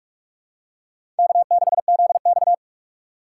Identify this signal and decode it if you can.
Morse code, heard as a keyed tone, about 41 wpm, 700 Hz. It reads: K6ZX
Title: K1USN SST QSO Practice - 45wpm